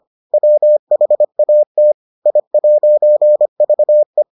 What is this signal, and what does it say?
Morse code, heard as a keyed tone, about 25 wpm, 600 Hz. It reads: WHAT I'VE